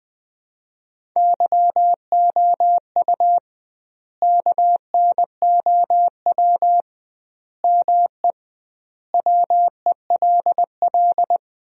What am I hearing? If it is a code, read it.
Morse code, heard as a keyed tone, about 20 wpm, 700 Hz. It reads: YOU KNOW ME WELL
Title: 2x Word-Spacing - Sentences from Top 400 Words - 20wpm